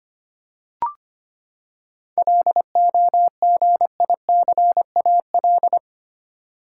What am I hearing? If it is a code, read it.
Morse code, heard as a keyed tone, about 25 wpm, 700 Hz. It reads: LOGICAL
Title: Seven Letter Words - 25wpm